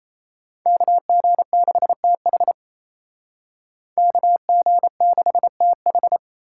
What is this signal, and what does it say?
Morse code, heard as a keyed tone, about 33 wpm, 700 Hz. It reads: KG6TH KG6TH
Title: Speed Racing - USA 2x2 Call Signs 28wpm